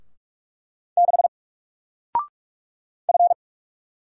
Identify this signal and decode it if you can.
Morse code, heard as a keyed tone, about 44 wpm, 700 Hz. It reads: 6 F
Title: Single Letter-Number 45wpm